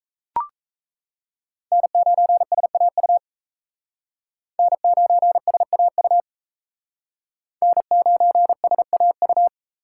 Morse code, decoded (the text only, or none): N9SAU N9SAU N9SAU